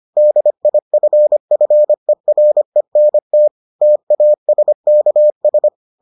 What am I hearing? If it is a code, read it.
Morse code, heard as a keyed tone, about 25 wpm, 600 Hz. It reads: DIFFERENT TASKS